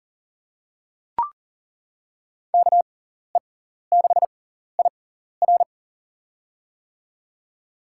Morse code, heard as a keyed tone, about 40 wpm, spaced as for 14 wpm, 700 Hz. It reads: KE6IR
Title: USA Call Signs - 5 Characters - 6x Character Spacing - 40wpm